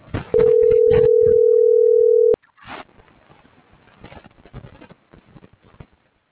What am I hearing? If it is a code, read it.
Ringback tone (North America), 440 + 480 Hz.